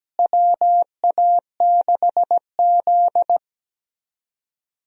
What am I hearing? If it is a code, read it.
Morse code, heard as a keyed tone, about 17 wpm, 700 Hz. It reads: WA6Z
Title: USA Call Signs - 4 Characters - Repeated 3x - 17wpm